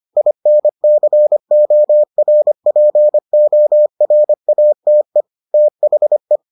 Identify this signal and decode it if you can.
Morse code, heard as a keyed tone, about 25 wpm, 600 Hz. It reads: INCORPORATE THE